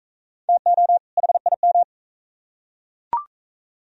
Morse code, heard as a keyed tone, about 44 wpm, 700 Hz. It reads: TO HIM